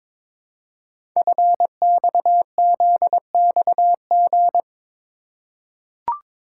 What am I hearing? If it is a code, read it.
Morse code, heard as a keyed tone, about 22 wpm, 700 Hz. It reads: FXZXG
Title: Code Groups - 5 Letters - 22wpm